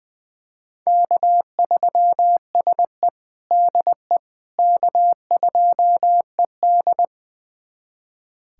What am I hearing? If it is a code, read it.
Morse code, heard as a keyed tone, about 20 wpm, 700 Hz. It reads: K3SE DE K2ED
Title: USA Callsign DE Callsign - 4 Characters - 20wpm